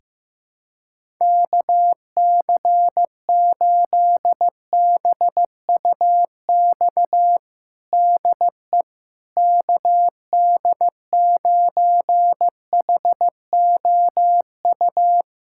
Morse code, heard as a keyed tone, about 15 wpm, 700 Hz. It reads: KC8BUX DE KD9HOU
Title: USA Callsign DE Callsign - 6 Characters - 15wpm